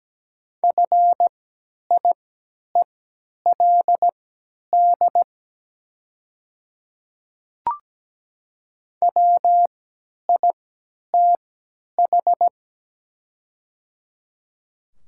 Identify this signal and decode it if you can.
Morse code, heard as a keyed tone, about 17 wpm, spaced as for 10 wpm, 700 Hz. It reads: FIELD WITH
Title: 3x Character Spacing - Top 500 Words - 17wpm